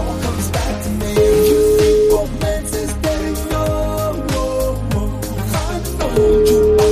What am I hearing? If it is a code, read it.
Ringback tone (Europe, 425 Hz), with noise.